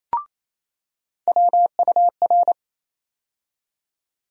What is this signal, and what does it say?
Morse code, heard as a keyed tone, about 28 wpm, 700 Hz. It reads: WUR
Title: USA Call Sign Postfixes - 3 Letters 28wpm